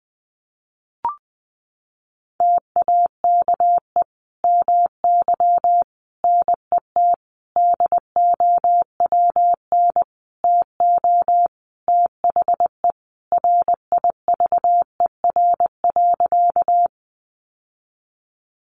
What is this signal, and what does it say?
Morse code, heard as a keyed tone, about 20 wpm, 700 Hz. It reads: TAKE MY NET DOWN TO THE RIVER.
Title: Sentences from Top 1400 Words 20wpm